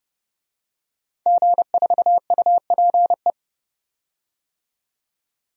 Morse code, heard as a keyed tone, about 30 wpm, 700 Hz. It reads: G4UPE
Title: DXCC-100 Call Signs - 5 Characters - 30wpm